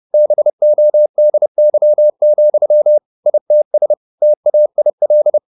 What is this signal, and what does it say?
Morse code, heard as a keyed tone, about 30 wpm, 600 Hz. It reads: BODY, ITS TAIL